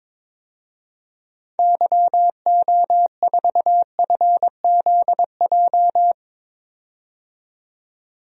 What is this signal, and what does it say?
Morse code, heard as a keyed tone, about 22 wpm, 700 Hz. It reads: YO4FZJ
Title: DXCC-100 Call Signs - 6 Characters - 22wpm